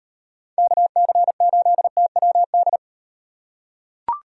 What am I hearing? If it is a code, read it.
Morse code, heard as a keyed tone, about 38 wpm, 700 Hz. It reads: KC8TWD